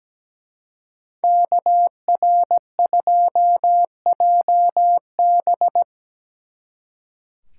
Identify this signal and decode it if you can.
Morse code, heard as a keyed tone, about 20 wpm, 700 Hz. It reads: KR2JB
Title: Speed Racing - USA 2x2 Call Signs 17wpm